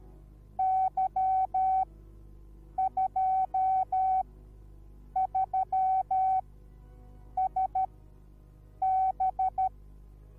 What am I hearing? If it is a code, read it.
Morse code, heard as a keyed tone, about 13 wpm, spaced as for 7 wpm, 760 Hz. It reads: Y23SB